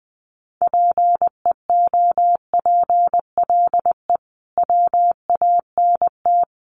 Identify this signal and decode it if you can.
Morse code, heard as a keyed tone, about 20 wpm, 700 Hz. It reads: PEOPLE WANT
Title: Sentences from Top 1500 Words 20wpm